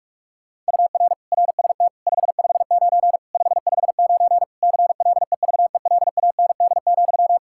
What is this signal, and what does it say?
Morse code, heard as a keyed tone, about 46 wpm, 700 Hz. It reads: UR RST 559 559 CLEVELAND,
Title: Practice QSO - Exchange 1 - Beginning - 45wpm